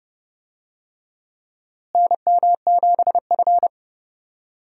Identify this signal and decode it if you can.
Morse code, heard as a keyed tone, about 32 wpm, 700 Hz. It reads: NM7F
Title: USA Call Signs - 4 Characters - Repeated 3x - 30wpm